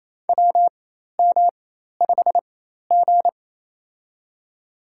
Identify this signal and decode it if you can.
Morse code, heard as a keyed tone, about 28 wpm, spaced as for 13 wpm, 700 Hz. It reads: WM5G
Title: USA Call Signs - 4 Characters - 4x Character Spacing - 28wpm